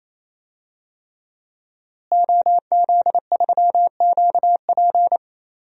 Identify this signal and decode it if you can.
Morse code, heard as a keyed tone, about 28 wpm, 700 Hz. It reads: OZ3QP
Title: DXCC-100 Call Signs - 5 Characters - Repeated 3x - 28wpm